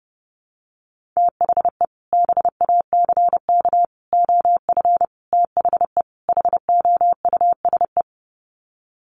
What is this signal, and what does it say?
Morse code, heard as a keyed tone, about 30 wpm, 700 Hz. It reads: THE BACK OF THE HOUSE
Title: Sets of 5 Words Encore2 30wpm